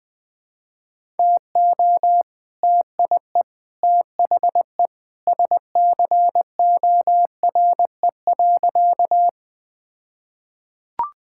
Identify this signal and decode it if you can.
Morse code, heard as a keyed tone, about 20 wpm, 700 Hz. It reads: TO TIE THE SCORE.
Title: All-Age Appropriate Jokes - 20wpm